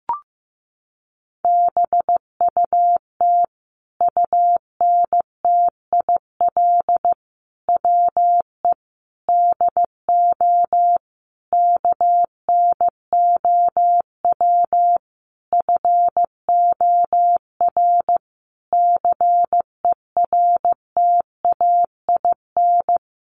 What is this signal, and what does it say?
Morse code, heard as a keyed tone, about 15 wpm, 700 Hz. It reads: BUT UNTIL WE DO KNOW FOR CERTAIN
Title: A Sound Of Thunder 15wpm